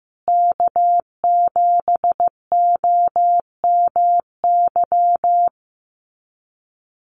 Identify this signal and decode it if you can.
Morse code, heard as a keyed tone, about 15 wpm, 700 Hz. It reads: K7OMY
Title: USA Call Signs Repeated 3x 15wpm